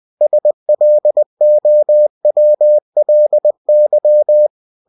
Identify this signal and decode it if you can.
Morse code, heard as a keyed tone, about 20 wpm, 600 Hz. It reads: SLOWLY
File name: cw-books / winnie-the-pooh / winnie-20wpm-page-0101